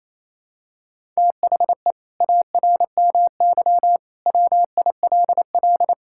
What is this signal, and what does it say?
Morse code, heard as a keyed tone, about 28 wpm, 700 Hz. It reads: THE ARMY WILL
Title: Sentences from Top 1000 Words - Review Part 6 - 28wpm